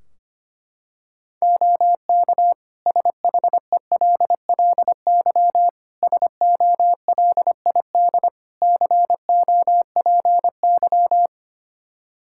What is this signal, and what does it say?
Morse code, heard as a keyed tone, about 25 wpm, 700 Hz. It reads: OK SHELLY SOLID COPY
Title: Practice QSO - Exchange 3 - More Info - 25wpm